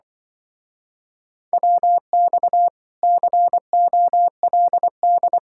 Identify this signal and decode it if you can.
Morse code, heard as a keyed tone, about 24 wpm, 700 Hz. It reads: WX COLD
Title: Speed-Racing Common QSO Phrases - 20wpm